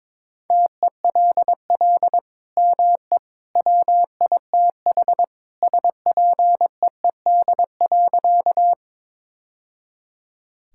Morse code, heard as a keyed tone, about 22 wpm, 700 Hz. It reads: TELL ME WITH SPEED.
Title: Sentences from Top 1000 Words - Review Part 5 - 22wpm